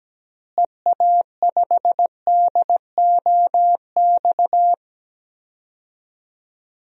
Morse code, heard as a keyed tone, about 17 wpm, 700 Hz. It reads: EA5DOX
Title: DXCC-100 Call Signs - 6 Characters - Repeated 2x - 17wpm